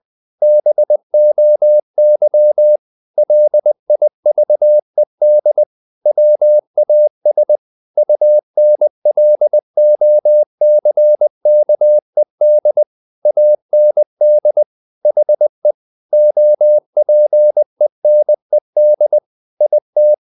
Morse code, heard as a keyed tone, about 20 wpm, 600 Hz. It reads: BOY LIVED WAS UNLOCKED AND HE OPENED IT
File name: the old man and the sea / the old man and the sea part 1 20wpm